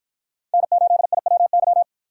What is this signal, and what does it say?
Morse code, heard as a keyed tone, about 51 wpm, 700 Hz. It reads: N8IWY